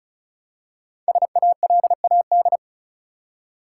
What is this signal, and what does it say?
Morse code, heard as a keyed tone, about 34 wpm, 700 Hz. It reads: SALAD
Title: Five Letter Words - 35wpm